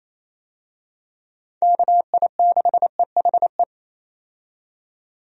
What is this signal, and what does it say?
Morse code, heard as a keyed tone, about 28 wpm, 700 Hz. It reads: KI6EHE
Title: USA Call Signs - 6 Characters - 28wpm